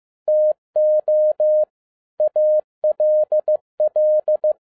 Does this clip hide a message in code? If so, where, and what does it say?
Yes: Morse code from 0.3 to 4.5 s, heard as a keyed tone, about 15 wpm, 610 Hz. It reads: TO ALL